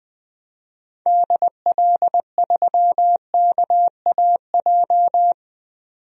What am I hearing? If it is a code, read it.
Morse code, heard as a keyed tone, about 20 wpm, 700 Hz. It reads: DL3KAJ
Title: DXCC-100 Call Signs - 6 Characters - 20wpm